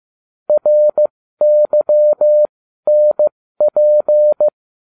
Morse code, heard as a keyed tone, about 15 wpm, spaced as for 12 wpm, 600 Hz. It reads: RYNP
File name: cw / ubacw20 2